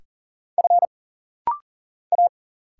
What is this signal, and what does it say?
Morse code, heard as a keyed tone, about 40 wpm, 700 Hz. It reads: F A
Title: Single Letter-Number Alt - Rapid-Fire 40wpm